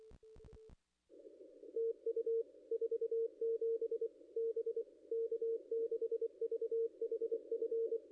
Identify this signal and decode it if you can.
Morse code, heard as a keyed tone, about 24 wpm, 460 Hz. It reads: TU 47 B K6VHF
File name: contests / ss-cw-k5tr-op-n5rz -0500